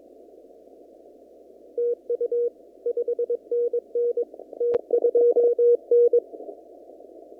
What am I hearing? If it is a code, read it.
Morse code, heard as a keyed tone, about 22 wpm, 440 Hz. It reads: TU 5NN T2N
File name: CW / kt5j wpx cw-op-k5pi